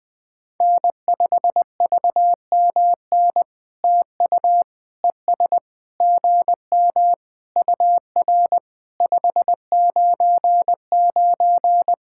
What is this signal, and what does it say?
Morse code, heard as a keyed tone, about 20 wpm, 700 Hz. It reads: N5VMN TU ES GM UR 599
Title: POTA QSO Practice - 20wpm